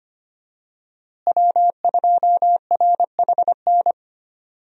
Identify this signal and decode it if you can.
Morse code, heard as a keyed tone, about 25 wpm, 700 Hz. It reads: W2RHN
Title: USA Call Signs - 5 Characters - Repeated 2x - 25wpm